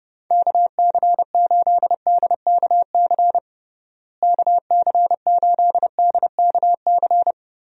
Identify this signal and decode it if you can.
Morse code, heard as a keyed tone, about 30 wpm, 700 Hz. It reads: KC8DKC KC8DKC
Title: USA Call Signs - 6 Characters - Repeated 3x - 30wpm